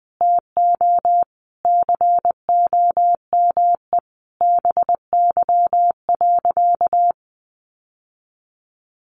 Morse code, heard as a keyed tone, about 20 wpm, 700 Hz. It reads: TO COME BY.
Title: Sentences from Top 1500 Words 20wpm